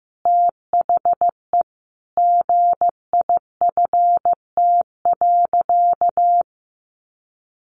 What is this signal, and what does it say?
Morse code, heard as a keyed tone, about 15 wpm, 700 Hz. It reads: THE GIFT.
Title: Easy Phrases 15wpm